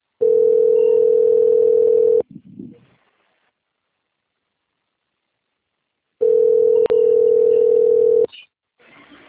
Ringback tone (North America), 440 + 480 Hz.
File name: out- - - - - .46193